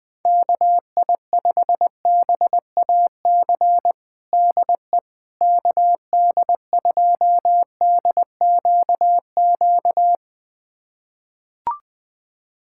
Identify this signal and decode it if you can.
Morse code, heard as a keyed tone, about 21 wpm, 700 Hz. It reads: KI5BAC DE KD2DQQ
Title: USA Callsign DE Callsign - 6 Characters - 20wpm